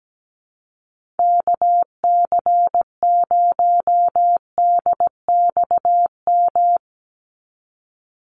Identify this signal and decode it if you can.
Morse code, heard as a keyed tone, about 17 wpm, 700 Hz. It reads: KC0DXM
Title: USA Call Signs 17wpm